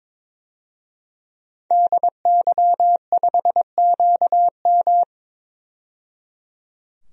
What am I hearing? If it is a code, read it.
Morse code, heard as a keyed tone, about 22 wpm, 700 Hz. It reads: DY5QM
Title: DXCC-100 Call Signs - 5 Characters - 22wpm